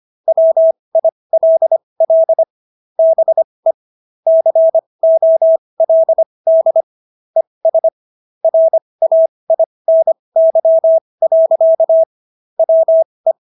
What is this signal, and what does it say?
Morse code, heard as a keyed tone, about 25 wpm, spaced as for 20 wpm, 650 Hz. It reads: WILL BE COLD ES RAINY. WE